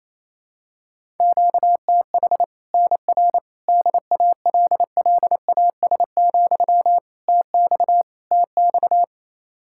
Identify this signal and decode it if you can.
Morse code, heard as a keyed tone, about 28 wpm, 700 Hz. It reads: QTH NR DALLAS, TX TX
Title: Practice QSO - Exchange 1 - Beginning - 28wpm